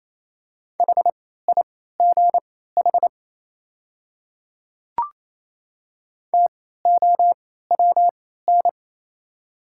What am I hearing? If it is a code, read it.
Morse code, heard as a keyed tone, about 28 wpm, spaced as for 16 wpm, 700 Hz. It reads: HIGH TOWN
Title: 3x Character Spacing - Top 500 Words - 28wpm